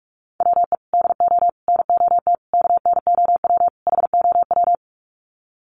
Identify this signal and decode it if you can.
Morse code, heard as a keyed tone, about 44 wpm, 700 Hz. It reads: WE DO NOT KNOW HOW
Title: Sets of 5 Words Encore 45wpm